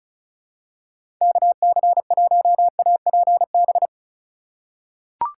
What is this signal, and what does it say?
Morse code, heard as a keyed tone, about 35 wpm, 700 Hz. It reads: KC1APB
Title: USA Call Signs - 6 Characters - 3x Character Spacing - 35wpm